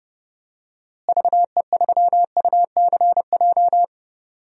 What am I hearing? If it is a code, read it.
Morse code, heard as a keyed tone, about 30 wpm, 700 Hz. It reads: VE3UCJ
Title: DXCC-100 Call Signs - 6 Characters - Repeated 3x - 30wpm